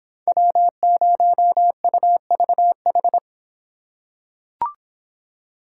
Morse code, heard as a keyed tone, about 25 wpm, 700 Hz. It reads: W0UVH